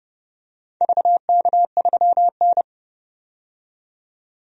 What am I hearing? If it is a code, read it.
Morse code, heard as a keyed tone, about 30 wpm, 700 Hz. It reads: VK3N
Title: DXCC-100 Call Signs - 4 Characters - 30wpm